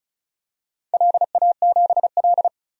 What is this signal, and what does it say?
Morse code, heard as a keyed tone, about 36 wpm, 700 Hz. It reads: LA7L
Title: DXCC-100 Call Signs - 4 Characters - 35wpm